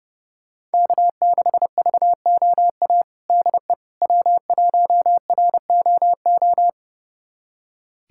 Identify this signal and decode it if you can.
Morse code, heard as a keyed tone, about 30 wpm, 700 Hz. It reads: K6VOA DE W1ROO
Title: USA Callsign DE Callsign - 5 Characters - 30wpm